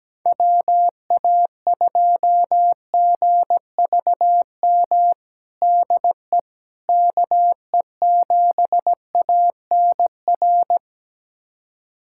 Morse code, heard as a keyed tone, about 17 wpm, 700 Hz. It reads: WA2GVM DE KE7ANR
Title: USA Callsign DE Callsign - 6 Characters - 17wpm